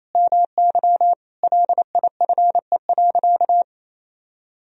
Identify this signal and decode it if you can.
Morse code, heard as a keyed tone, about 28 wpm, 700 Hz. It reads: MY LIFE.
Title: Sentences from Top 1000 Words - Review Part 1 - 28wpm